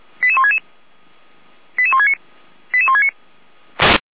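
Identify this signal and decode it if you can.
Five-tone selective call (ZVEI), heard as quick runs of five tones, tones of about 70 ms. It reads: 88166 88169 88169
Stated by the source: Received on an SDR, ZVEI five-tone selective call.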